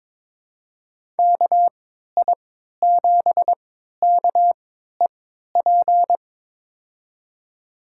Morse code, heard as a keyed tone, about 22 wpm, spaced as for 12 wpm, 700 Hz. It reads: KI7KEP
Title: USA Call Signs - 6 Characters - 3x Character Spacing - 22wpm